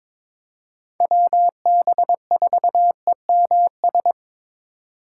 Morse code, heard as a keyed tone, about 22 wpm, 700 Hz. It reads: WB4EMS
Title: USA Call Signs - 6 Characters - 6x Character Spacing - 22wpm